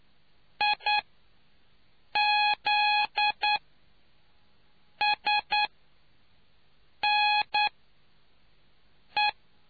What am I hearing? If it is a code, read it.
Morse code, heard as a keyed tone, about 9 wpm, spaced as for 5 wpm, 840 Hz. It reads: IZSNE